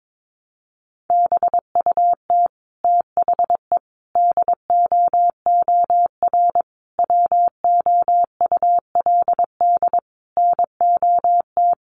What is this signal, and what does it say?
Morse code, heard as a keyed tone, about 22 wpm, 700 Hz. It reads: BUT THE DOOR WOULD NOT
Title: Sentences from Top 600 Words 22wpm